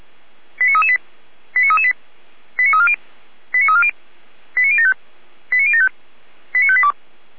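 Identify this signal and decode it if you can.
Five-tone selective call (ZVEI), heard as quick runs of five tones, tones of about 70 ms. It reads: 89338 89338 89340 89340 89075 89075 89582
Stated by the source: ZVEI five-tone selective call received on an SDR.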